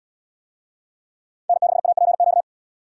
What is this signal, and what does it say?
Morse code, heard as a keyed tone, about 73 wpm, 700 Hz. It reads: N6APQ